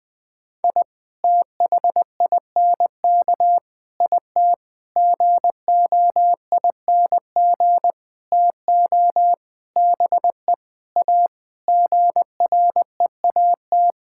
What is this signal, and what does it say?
Morse code, heard as a keyed tone, about 20 wpm, 700 Hz. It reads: I THINK IT GOING TO BE A GREAT
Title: Sentences with 6-Word Phrases - 20wpm